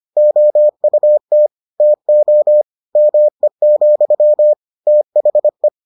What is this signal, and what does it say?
Morse code, heard as a keyed tone, about 25 wpm, 600 Hz. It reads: OUT TO ME, THE